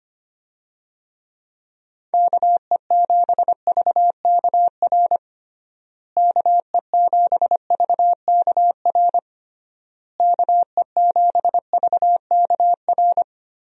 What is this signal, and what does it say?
Morse code, heard as a keyed tone, about 25 wpm, 700 Hz. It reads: KE7VKR KE7VKR KE7VKR
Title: USA Call Signs - 6 Characters - Repeated 3x - 25wpm